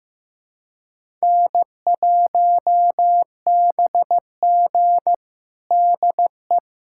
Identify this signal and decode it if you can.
Morse code, heard as a keyed tone, about 15 wpm, 700 Hz. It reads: N1BG DE
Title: USA Callsign DE Callsign - 4 Characters - 15wpm